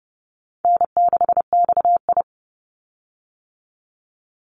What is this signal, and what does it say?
Morse code, heard as a keyed tone, about 30 wpm, 700 Hz. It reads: N6XI
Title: Easy Call Signs - 30wpm